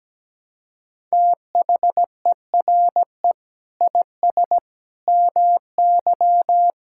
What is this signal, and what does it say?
Morse code, heard as a keyed tone, about 17 wpm, 700 Hz. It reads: THERE IS MY
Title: Sentences from Top 1000 Words - Review Part 6 - 17wpm